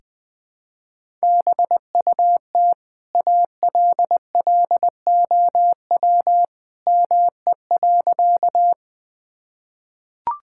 Morse code, heard as a keyed tone, about 20 wpm, 700 Hz. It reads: BUT ALLOW ME.
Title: Sentences from Top 1000 Words - Review Part 5 - 20wpm